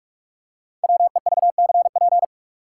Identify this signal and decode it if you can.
Morse code, heard as a keyed tone, about 42 wpm, 700 Hz. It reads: WEVCP
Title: Code Groups - 5 Letter:Numbers - 45wpm